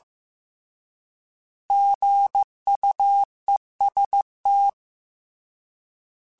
Morse code, heard as a keyed tone, about 15 wpm, 800 Hz. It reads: GUEST
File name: Alphabetical / Alphabetical mp3s / Gs @13